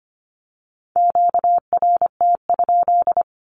Morse code, heard as a keyed tone, about 25 wpm, 700 Hz. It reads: QRT?
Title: Q-Codes with Spoken Meaning - 25wpm